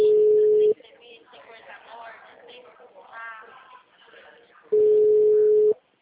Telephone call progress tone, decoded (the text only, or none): ringback tone